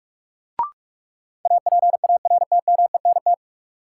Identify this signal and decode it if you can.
Morse code, heard as a keyed tone, about 42 wpm, 700 Hz. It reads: APARTMENT
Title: Sentences from Top 900 Words - New Word Drill - 45wpm